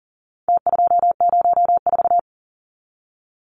Morse code, heard as a keyed tone, about 40 wpm, 700 Hz. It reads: T204
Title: Numbers - Contest Style - 4-Digit 40wpm